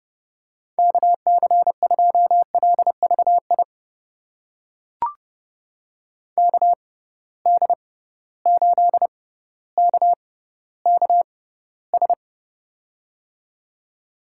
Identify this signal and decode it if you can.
Morse code, heard as a keyed tone, about 31 wpm, spaced as for 16 wpm, 700 Hz. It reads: KC2LVI KD8KKS
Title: USA Call Signs - 6x Character Spacing - 30wpm